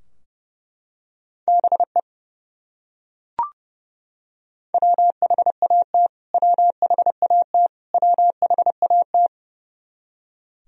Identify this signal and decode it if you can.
Morse code, heard as a keyed tone, about 29 wpm, 700 Hz. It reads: BE WHAT WHAT WHAT
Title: Top 100 Words Repeated 30wpm